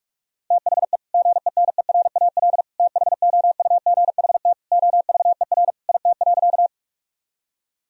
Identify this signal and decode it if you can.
Morse code, heard as a keyed tone, about 47 wpm, 700 Hz. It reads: THE GENERAL THOUGHT OVER IT.
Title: Sentences from Top 1000 Words - Review Part 3 - 45wpm